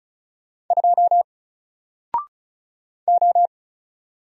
Morse code, heard as a keyed tone, about 35 wpm, 700 Hz. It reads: 2 O
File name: Single-Letter-Number-35wpm